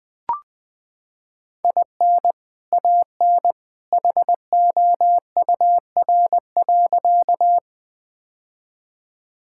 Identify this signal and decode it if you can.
Morse code, heard as a keyed tone, about 21 wpm, 700 Hz. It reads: IN AN HOUR.
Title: Sentences with 2-Word Phrases - 20wpm